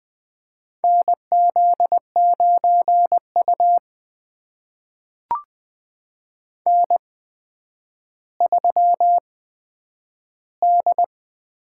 Morse code, heard as a keyed tone, about 20 wpm, spaced as for 10 wpm, 700 Hz. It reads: NZ9U N3D
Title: USA Call Signs - 4 Characters - 8x Character Spacing - 20wpm